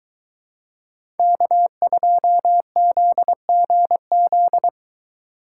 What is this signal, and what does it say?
Morse code, heard as a keyed tone, about 23 wpm, 700 Hz. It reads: K2ZGZ